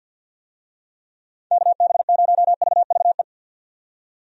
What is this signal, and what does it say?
Morse code, heard as a keyed tone, about 51 wpm, 700 Hz. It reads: KB0VVE
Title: USA Call Signs - 6 Characters - 50wpm